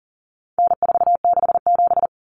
Morse code, heard as a keyed tone, about 40 wpm, 700 Hz. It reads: N467
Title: Common Cut Numbers - 4-Digit 40wpm